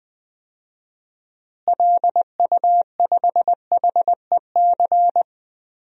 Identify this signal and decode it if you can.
Morse code, heard as a keyed tone, about 21 wpm, 700 Hz. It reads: LU5HEC
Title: DXCC-100 Call Signs - 6 Characters - Repeated 2x - 20wpm